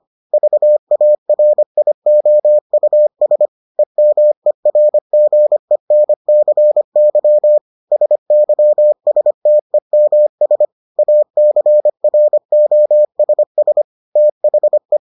Morse code, heard as a keyed tone, about 25 wpm, 600 Hz. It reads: VARIOUS EMERGENCY SYSTEMS ACROSS THE